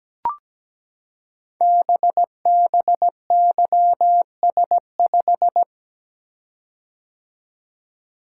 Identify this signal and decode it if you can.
Morse code, heard as a keyed tone, about 17 wpm, 700 Hz. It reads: BBYS5
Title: Code Groups - 5 Letter:Numbers - 17wpm